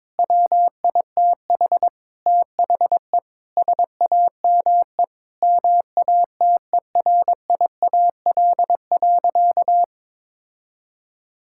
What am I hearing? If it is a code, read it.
Morse code, heard as a keyed tone, about 22 wpm, 700 Hz. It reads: WITH THE SAME MATERIAL.
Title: Sentences from Top 1000 Words - Review Part 5 - 22wpm